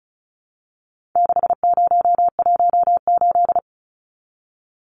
Morse code, heard as a keyed tone, about 34 wpm, 700 Hz. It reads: 6018
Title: Numbers - 4-Digit 35wpm